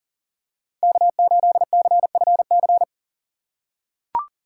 Morse code, heard as a keyed tone, about 40 wpm, 700 Hz. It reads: K8CFC
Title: USA Call Signs - 5 Characters - 4x Character Spacing - 40wpm